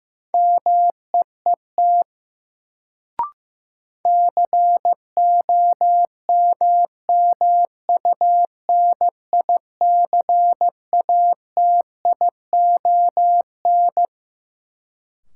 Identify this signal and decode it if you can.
Morse code, heard as a keyed tone, about 15 wpm, 700 Hz. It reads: MEET COMMUNICATION
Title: Sentences from Top 1000 Words - Review Part 6 - New Word Drill - 15wpm